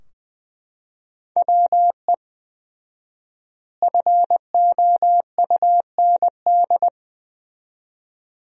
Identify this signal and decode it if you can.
Morse code, heard as a keyed tone, about 20 wpm, 700 Hz. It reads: WE FOUND
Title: 4x Word-Spacing - Sentences from Top 300 Words - 20wpm